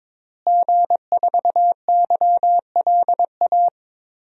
Morse code, heard as a keyed tone, about 22 wpm, 700 Hz. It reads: G4YLA
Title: DXCC-100 Call Signs - 5 Characters - Repeated 3x - 22wpm